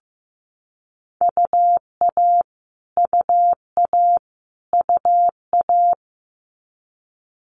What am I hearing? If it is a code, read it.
Morse code, heard as a keyed tone, about 15 wpm, 700 Hz. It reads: UA UA UA
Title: Multi-letter Phonemes Repeated 3x 15wpm